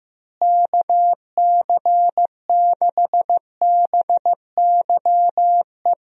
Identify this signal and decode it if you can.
Morse code, heard as a keyed tone, about 15 wpm, 700 Hz. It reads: KC6BYE
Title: USA Call Signs - 6 Characters - Repeated 3x - 15wpm